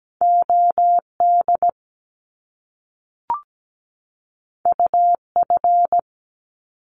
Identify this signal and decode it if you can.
Morse code, heard as a keyed tone, about 17 wpm, 700 Hz. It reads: OD UF
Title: Top Two Letter Combinations in Words 17wpm